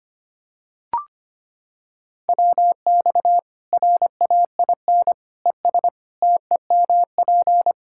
Morse code, heard as a keyed tone, about 25 wpm, 700 Hz. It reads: WX RAIN ES TEMP
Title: K1USN SST QSO Practice 25wpm